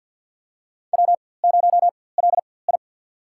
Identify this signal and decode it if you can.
Morse code, heard as a keyed tone, about 49 wpm, spaced as for 23 wpm, 700 Hz. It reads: W0LI
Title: USA Call Signs - 4x Character Spacing - 50wpm